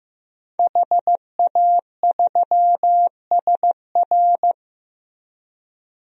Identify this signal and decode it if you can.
Morse code, heard as a keyed tone, about 15 wpm, 700 Hz. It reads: HA3SR
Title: DXCC-100 Call Signs - 5 Characters - Repeated 3x - 15wpm